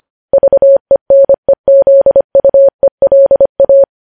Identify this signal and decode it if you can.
Morse code, heard as a keyed tone, about 25 wpm, 550 Hz. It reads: VENEZUELA
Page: news-headlines-in-morse-code-at-25-wpm